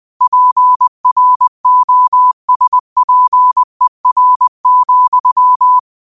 Morse code, heard as a keyed tone, about 20 wpm, 1000 Hz. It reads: PROSPER,